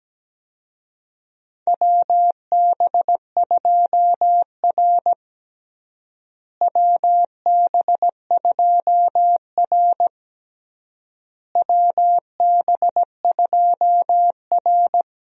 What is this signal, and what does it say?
Morse code, heard as a keyed tone, about 17 wpm, 700 Hz. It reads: WB2R WB2R WB2R
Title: USA Call Signs - 4 Characters - Repeated 3x - 17wpm